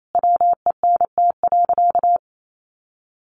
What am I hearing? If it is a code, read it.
Morse code, heard as a keyed tone, about 28 wpm, 700 Hz. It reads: WENT.
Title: Sentences from Top 200 Words 28wpm